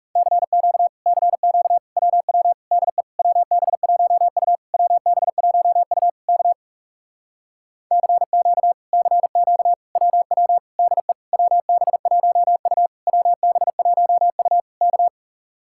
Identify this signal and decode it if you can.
Morse code, heard as a keyed tone, about 40 wpm, 700 Hz. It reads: CQ CQ WW DE WB1U WB1U K CQ CQ WW DE WB1U WB1U K